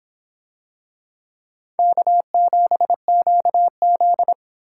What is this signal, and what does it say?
Morse code, heard as a keyed tone, about 24 wpm, 700 Hz. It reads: K7QZ